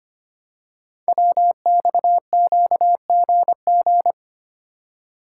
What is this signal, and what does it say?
Morse code, heard as a keyed tone, about 25 wpm, 700 Hz. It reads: WXQGG
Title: Code Groups - 5 Letters - 25wpm